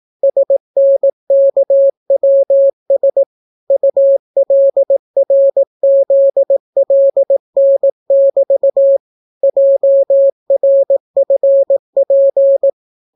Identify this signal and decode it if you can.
Morse code, heard as a keyed tone, about 18 wpm, 550 Hz. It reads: SNKWS ULRZLN<BT> JRFP